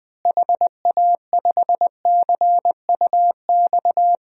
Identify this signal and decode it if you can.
Morse code, heard as a keyed tone, about 21 wpm, 700 Hz. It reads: HA5CUX
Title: DXCC-100 Call Signs - 6 Characters - Repeated 2x - 20wpm